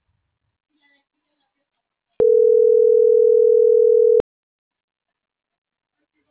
Ringback tone (North America), 440 + 480 Hz.